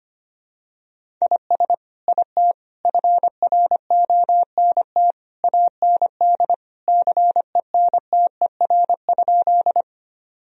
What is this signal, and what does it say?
Morse code, heard as a keyed tone, about 25 wpm, 700 Hz. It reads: IS IT FRONT AND CENTER?
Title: Sentences from Top 1000 Words - Review Part 2 - 25wpm